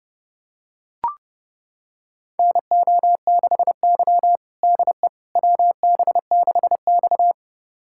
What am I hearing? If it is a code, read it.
Morse code, heard as a keyed tone, about 32 wpm, 700 Hz. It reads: NO6Y DE WB6X
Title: USA Callsign DE Callsign - 4 Characters - 30wpm